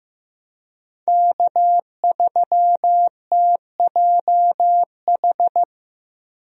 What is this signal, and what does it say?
Morse code, heard as a keyed tone, about 15 wpm, 700 Hz. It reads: K3TJH
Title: USA Call Signs - 5 Characters - Repeated 3x - 15wpm